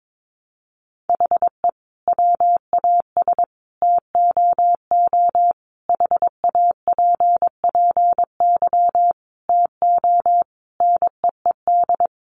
Morse code, heard as a keyed tone, about 22 wpm, 700 Hz. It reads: HE WAS TOO HAPPY TO NEED
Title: Sentences from Top 1000 Words 22wpm